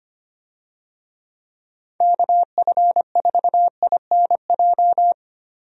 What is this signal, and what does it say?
Morse code, heard as a keyed tone, about 25 wpm, 700 Hz. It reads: KF4INJ
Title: USA Call Signs - 6 Characters - Repeated 3x - 25wpm